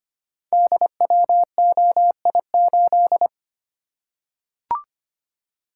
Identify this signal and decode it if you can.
Morse code, heard as a keyed tone, about 25 wpm, 690 Hz. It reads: DWOI8
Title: Code Groups - 5 Letter:Numbers - 25wpm